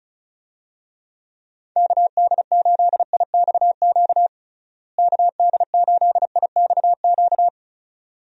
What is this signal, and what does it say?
Morse code, heard as a keyed tone, about 34 wpm, 700 Hz. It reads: KD8IXQ KD8IXQ
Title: USA Call Signs - 6 Characters - Repeated 2x - 35wpm